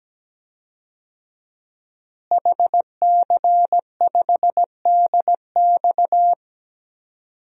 Morse code, heard as a keyed tone, about 17 wpm, 700 Hz. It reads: HC5DX
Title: DXCC-100 Call Signs - 5 Characters - Repeated 3x - 17wpm